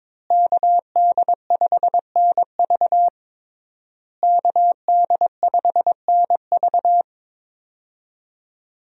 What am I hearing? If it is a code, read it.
Morse code, heard as a keyed tone, about 22 wpm, 700 Hz. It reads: KD5NV KD5NV
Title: USA Call Signs - 5 Characters - Repeated 3x - 22wpm